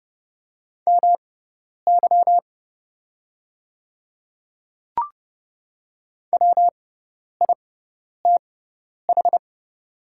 Morse code, heard as a keyed tone, about 30 wpm, spaced as for 10 wpm, 700 Hz. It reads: MY WITH
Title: 6x Character Spacing - Top 100 Words - 30wpm